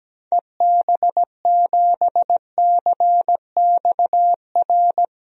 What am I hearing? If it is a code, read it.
Morse code, heard as a keyed tone, about 17 wpm, 700 Hz. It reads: EB7CXR
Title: DXCC-100 Call Signs - 6 Characters - Repeated 3x - 17wpm